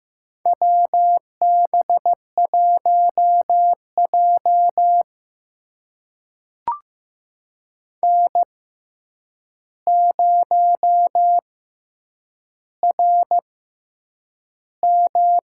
Morse code, heard as a keyed tone, about 15 wpm, spaced as for 8 wpm, 700 Hz. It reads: WB1J N0RM
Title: USA Call Signs - 4 Characters - 6x Character Spacing - 15wpm